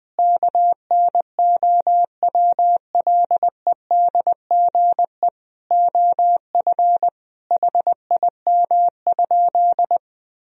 Morse code, heard as a keyed tone, about 20 wpm, 700 Hz. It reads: KNOWLEDGE OF HIM?
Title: Sentences from Top 1000 Words - Review Part 3 - 20wpm